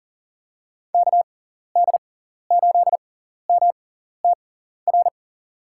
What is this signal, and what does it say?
Morse code, heard as a keyed tone, about 40 wpm, spaced as for 14 wpm, 700 Hz. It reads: KD8MTR
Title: USA Call Signs - 6 Characters - 6x Character Spacing - 40wpm